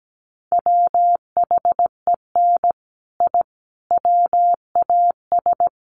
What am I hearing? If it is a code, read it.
Morse code, heard as a keyed tone, about 17 wpm, 700 Hz. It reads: WHEN I WAS
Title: Sets of 5 Words Encore 17wpm